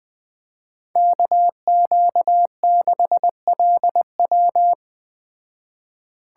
Morse code, heard as a keyed tone, about 20 wpm, 700 Hz. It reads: KQ6LW
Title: Speed Racing - USA Call Signs 17wpm